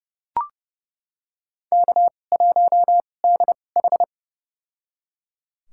Morse code, heard as a keyed tone, about 30 wpm, spaced as for 22 wpm, 700 Hz. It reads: K1DH
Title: USA Call Signs - 2x Character Spacing - 30wpm